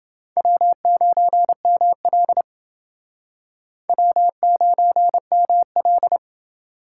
Morse code, heard as a keyed tone, about 29 wpm, 700 Hz. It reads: W9ML W9ML